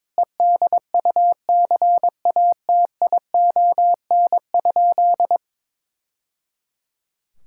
Morse code, heard as a keyed tone, about 22 wpm, 700 Hz. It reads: EDUCATION?
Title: Sentences from Top 1000 Words - Review Part 6 - 22wpm